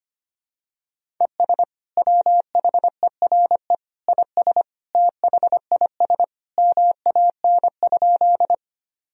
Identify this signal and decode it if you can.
Morse code, heard as a keyed tone, about 25 wpm, 700 Hz. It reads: ES WHERE IS THIS MAN?
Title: Sentences from Top 200 Words and Abbreviations 25wpm